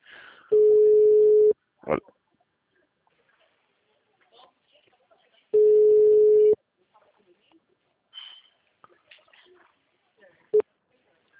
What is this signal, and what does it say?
Ringback tone (Europe, 425 Hz), with speech.
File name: out- -2010- - - .4